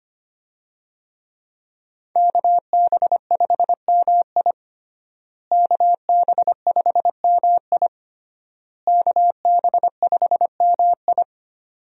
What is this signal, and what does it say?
Morse code, heard as a keyed tone, about 25 wpm, 700 Hz. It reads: KB5MI KB5MI KB5MI
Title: USA Call Signs - 5 Characters - Repeated 3x - 25wpm